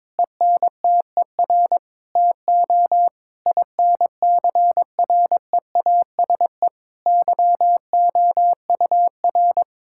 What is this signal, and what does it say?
Morse code, heard as a keyed tone, about 22 wpm, 700 Hz. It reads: ENTER TO INCREASE YOUR
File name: practice / 22wpm / Sentences with 8-Word Phrases - 22wpm